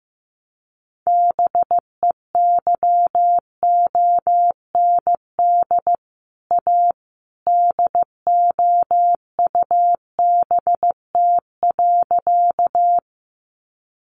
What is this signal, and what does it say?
Morse code, heard as a keyed tone, about 15 wpm, 700 Hz. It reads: BEYOND A DOUBT.
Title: Sentences from Top 1500 Words 15wpm